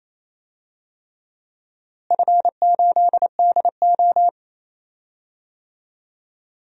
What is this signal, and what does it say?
Morse code, heard as a keyed tone, about 28 wpm, 700 Hz. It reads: F8DO
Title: DXCC-100 Call Signs - 4 Characters - Repeated 3x - 28wpm